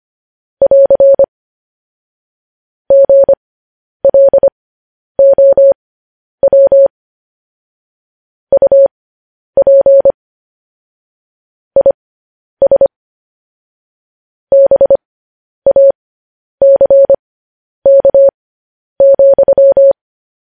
Morse code, heard as a keyed tone, about 25 wpm, spaced as for 10 wpm, 550 Hz. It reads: <AR> GLOW UP IS BACK,
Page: news headlines in morse code at 10 wpm